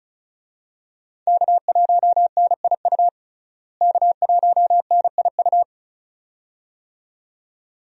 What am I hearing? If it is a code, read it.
Morse code, heard as a keyed tone, about 36 wpm, 700 Hz. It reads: K1NIU K1NIU
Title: USA Call Signs - 5 Characters - Repeated 2x - 35wpm